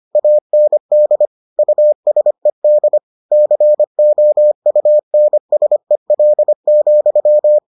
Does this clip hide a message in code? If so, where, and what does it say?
Yes: Morse code from 0.2 to 7.6 s, heard as a keyed tone, about 25 wpm, 600 Hz. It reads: AND USED COUNSEL,